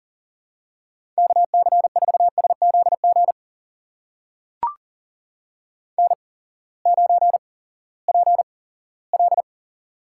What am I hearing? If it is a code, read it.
Morse code, heard as a keyed tone, about 43 wpm, spaced as for 20 wpm, 700 Hz. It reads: KC4SZG N9PL
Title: USA Call Signs - 8x Character Spacing - 40wpm